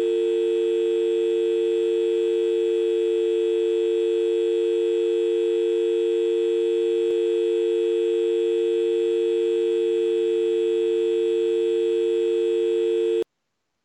Dial tone (UK), 350 + 450 Hz.